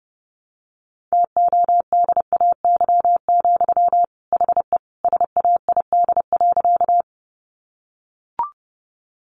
Morse code, heard as a keyed tone, about 30 wpm, 700 Hz. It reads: TODAY, HE SAID.
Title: Sentences from Top 600 Words 30wpm